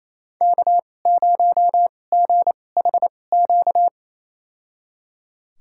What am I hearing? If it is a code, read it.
Morse code, heard as a keyed tone, about 28 wpm, spaced as for 20 wpm, 700 Hz. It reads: K0GHQ
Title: USA Call Signs - 5 Characters - 2x Character Spacing - 28wpm